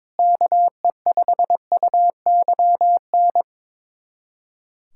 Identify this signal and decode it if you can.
Morse code, heard as a keyed tone, about 22 wpm, 700 Hz. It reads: KE5UYN
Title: USA Callsign DE Callsign 22wpm